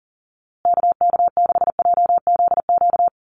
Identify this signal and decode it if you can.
Morse code, heard as a keyed tone, about 40 wpm, 700 Hz. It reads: KK6JZQ
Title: USA Call Signs Repeated 2x 40wpm